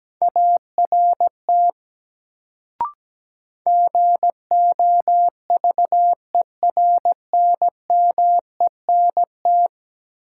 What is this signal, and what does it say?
Morse code, heard as a keyed tone, about 17 wpm, 700 Hz. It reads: ART GOVERNMENT
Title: Sentences from Top 600 Words - New Word Drill - 17wpm